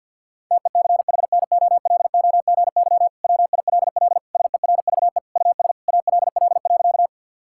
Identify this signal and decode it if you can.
Morse code, heard as a keyed tone, about 50 wpm, 700 Hz. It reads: TECHNOLOGY WILL SERVE US ALL.
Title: Sentences from Top 1000 Words - Review Part 5 - 50wpm